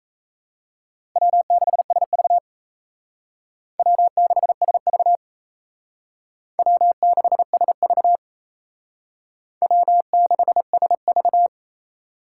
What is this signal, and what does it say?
Morse code, heard as a keyed tone, about 35 wpm, 700 Hz. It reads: W6SV W6SV W6SV W6SV